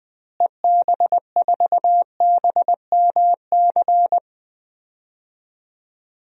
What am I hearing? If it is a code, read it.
Morse code, heard as a keyed tone, about 20 wpm, 700 Hz. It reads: EB4BMC
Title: DXCC-100 Call Signs - 6 Characters - Repeated 3x - 20wpm